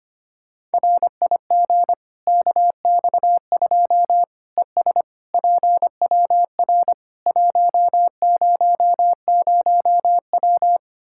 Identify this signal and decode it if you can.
Morse code, heard as a keyed tone, about 25 wpm, 700 Hz. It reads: RIG KX2 ES PWR 100W
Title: Practice QSO - Exchange 2 - Info - 25wpm